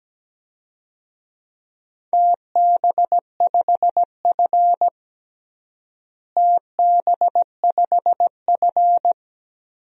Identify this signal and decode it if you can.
Morse code, heard as a keyed tone, about 17 wpm, 700 Hz. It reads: TB5F TB5F
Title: DXCC-100 Call Signs - 4 Characters - Repeated 3x - 17wpm